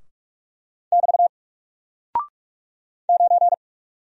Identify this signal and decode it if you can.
Morse code, heard as a keyed tone, about 44 wpm, 700 Hz. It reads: <BT> 9
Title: Single Character (Letter:Number:Punctuation:Prosign) Alt 45wpm